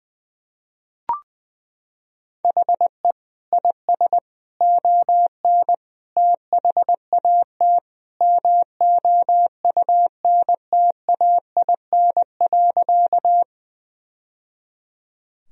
Morse code, heard as a keyed tone, about 20 wpm, 700 Hz. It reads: HE IS ON THAT MOUNTAIN.
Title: Sentences from Top 1000 Words - Review Part 1 - 20wpm